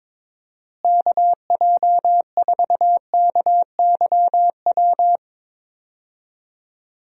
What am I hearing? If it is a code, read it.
Morse code, heard as a keyed tone, about 22 wpm, 700 Hz. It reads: KJ4KYW
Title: USA Call Signs - 6 Characters - Repeated 3x - 22wpm